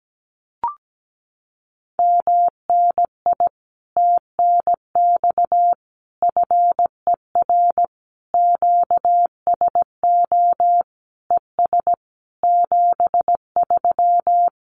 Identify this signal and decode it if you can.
Morse code, heard as a keyed tone, about 17 wpm, 700 Hz. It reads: MNI TNX FER QSO ES 73
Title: Common QSO Phrases Encore 17wpm